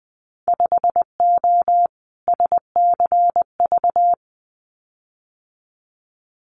Morse code, heard as a keyed tone, about 20 wpm, 700 Hz. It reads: 5O SCV
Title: Winter Field Day Exchanges Encore 20wpm